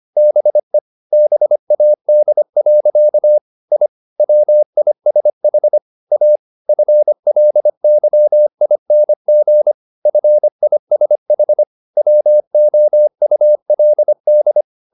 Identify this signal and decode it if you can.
Morse code, heard as a keyed tone, about 25 wpm, 600 Hz. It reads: BE BAD. I WISH A FLYING FISH WOULD
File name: the old man and the sea part 1 25wpm